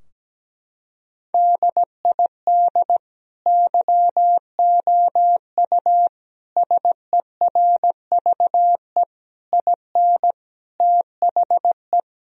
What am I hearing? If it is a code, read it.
Morse code, heard as a keyed tone, about 17 wpm, 700 Hz. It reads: DID YOU SERVE IN THE
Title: Sentences from Top 1000 Words - Review Part 4 - 17wpm